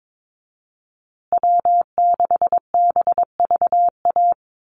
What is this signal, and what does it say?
Morse code, heard as a keyed tone, about 22 wpm, 700 Hz. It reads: W6BVA
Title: USA Call Signs Repeated 2x 22wpm